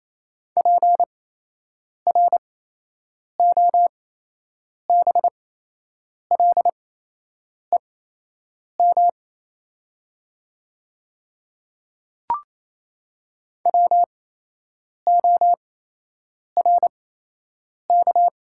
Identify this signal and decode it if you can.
Morse code, heard as a keyed tone, about 27 wpm, spaced as for 8 wpm, 700 Hz. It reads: PROBLEM WORK
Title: 8x Character Spacing - Top 500 Words - 28wpm